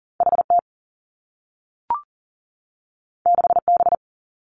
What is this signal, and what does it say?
Morse code, heard as a keyed tone, about 39 wpm, 700 Hz. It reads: H T 6 B
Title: Two Challenging Letter-Number Combinations 40wpm